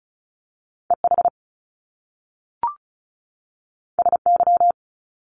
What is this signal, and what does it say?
Morse code, heard as a keyed tone, about 36 wpm, 700 Hz. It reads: EH SY
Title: Top Two Letter Combinations in Words 35wpm